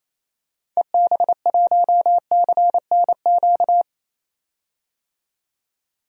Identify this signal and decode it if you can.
Morse code, heard as a keyed tone, about 28 wpm, 700 Hz. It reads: EB1CNQ
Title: DXCC-100 Call Signs - 6 Characters - 28wpm